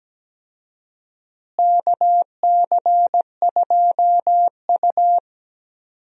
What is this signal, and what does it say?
Morse code, heard as a keyed tone, about 17 wpm, 700 Hz. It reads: KC2U
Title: USA Call Signs - 4 Characters - Repeated 2x - 17wpm